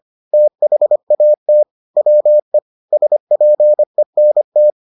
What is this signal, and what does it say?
Morse code, heard as a keyed tone, about 25 wpm, 600 Hz. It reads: THAT WE SPENT